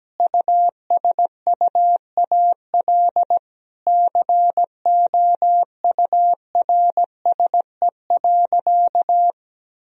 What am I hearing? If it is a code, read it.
Morse code, heard as a keyed tone, about 17 wpm, 700 Hz. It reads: USUAL COURSE.
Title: Sentences from Top 1000 Words - Review Part 2 - 17wpm